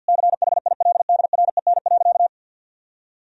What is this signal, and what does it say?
Morse code, heard as a keyed tone, about 50 wpm, 700 Hz. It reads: CHILDREN.
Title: Sentences from Top 1000 Words - Review Part 3 - 50wpm